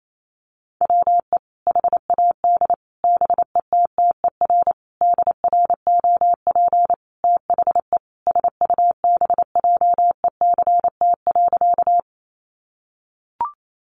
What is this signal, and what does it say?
Morse code, heard as a keyed tone, about 28 wpm, 700 Hz. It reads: WE HAD BETTER DROP THE SUBJECT.
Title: Sentences from Top 1400 Words 28wpm